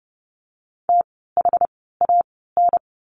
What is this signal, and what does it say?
Morse code, heard as a keyed tone, about 32 wpm, 700 Hz. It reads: T H A N
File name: Lesson 16f - Letter c - Words 30wpm